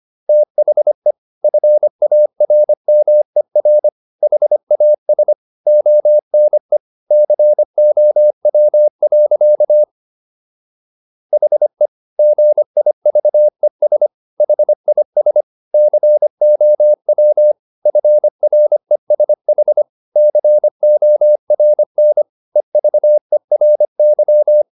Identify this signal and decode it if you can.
Morse code, heard as a keyed tone, about 25 wpm, 600 Hz. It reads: THE FARMER HAS ONE COW. HE GIVES HIS COW FRESH CORN EVERY